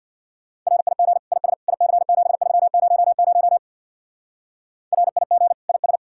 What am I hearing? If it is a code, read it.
Morse code, heard as a keyed tone, about 64 wpm, 700 Hz. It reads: RIG IS IC7300 RIG IS